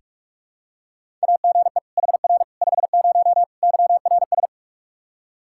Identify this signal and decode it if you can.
Morse code, heard as a keyed tone, about 46 wpm, 700 Hz. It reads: AGE HR 50 YRS
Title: K1USN SST QSO Practice 45wpm